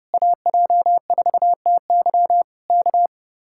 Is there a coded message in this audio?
Yes: Morse code from 0.1 to 3.1 s, heard as a keyed tone, about 30 wpm, 700 Hz. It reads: AJ4TY K